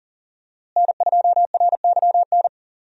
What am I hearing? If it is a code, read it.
Morse code, heard as a keyed tone, about 40 wpm, 700 Hz. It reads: N2RYN
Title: USA Call Signs - 5 Characters - Repeated 3x - 40wpm